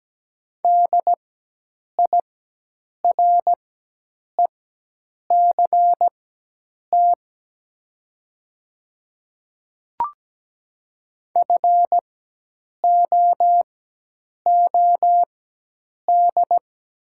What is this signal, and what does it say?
Morse code, heard as a keyed tone, about 17 wpm, spaced as for 8 wpm, 690 Hz. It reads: DIRECT FOOD
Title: 4x Character Spacing - Top 500 Words - 17wpm